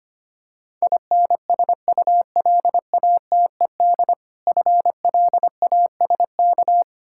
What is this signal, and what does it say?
Morse code, heard as a keyed tone, about 26 wpm, 700 Hz. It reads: INSULATED FLASK
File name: practice / 25wpm / Difficult Words - Speed-Racing with Hints - 25wpm